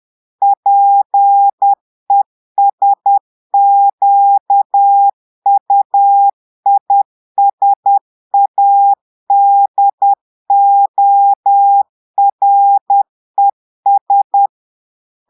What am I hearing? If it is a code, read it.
Morse code, heard as a keyed tone, about 10 wpm, 800 Hz. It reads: PESQUISADORES
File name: Portugais - texte en clair 1 - vitesse - 10 mots minute